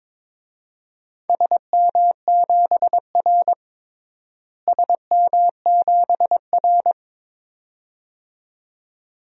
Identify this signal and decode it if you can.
Morse code, heard as a keyed tone, about 21 wpm, 700 Hz. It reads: SM7R SM7R
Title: DXCC-100 Call Signs - 4 Characters - Repeated 2x - 22wpm